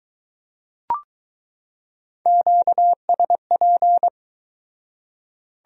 Morse code, heard as a keyed tone, about 22 wpm, 700 Hz. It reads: QSP